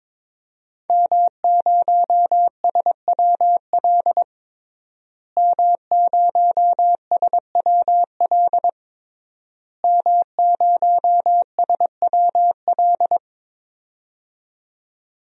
Morse code, heard as a keyed tone, about 22 wpm, 700 Hz. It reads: M0SWL M0SWL M0SWL
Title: DXCC-100 Call Signs - 5 Characters - Repeated 3x - 22wpm